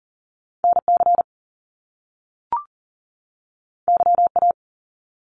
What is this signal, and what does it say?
Morse code, heard as a keyed tone, about 40 wpm, 700 Hz. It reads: NC YA
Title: Top Two Letter Combinations in Words 40wpm